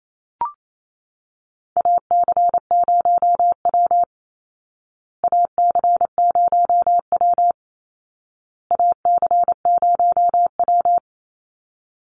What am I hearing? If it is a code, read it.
Morse code, heard as a keyed tone, about 28 wpm, 700 Hz. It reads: AC0W AC0W AC0W
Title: Call Signs Repeated 2x - 28wpm